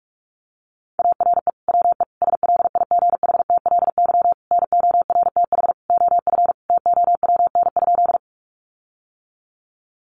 Morse code, heard as a keyed tone, about 44 wpm, 700 Hz. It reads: ARE WE SLIGHTLY NORTH OF TOWN?
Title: Sentences from Top 1300 Words 45wpm